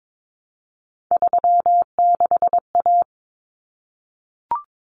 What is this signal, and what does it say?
Morse code, heard as a keyed tone, about 22 wpm, 700 Hz. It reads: 36A
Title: Common Cut Numbers - 3-Digit 22wpm